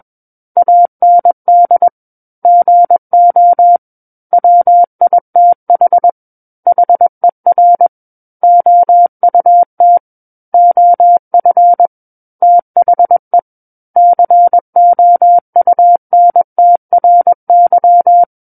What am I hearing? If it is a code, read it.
Morse code, heard as a keyed tone, about 21 wpm, 700 Hz. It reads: AND GO WITH HER OUT OF THE COUNTRY
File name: Book One / 12-Chapter 12 - 21 WPM